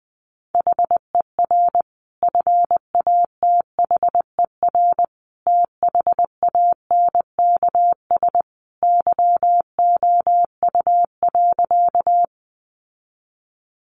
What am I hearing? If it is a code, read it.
Morse code, heard as a keyed tone, about 20 wpm, 700 Hz. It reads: HER FATHER THANKS YOU.
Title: Sentences from Top 600 Words 20wpm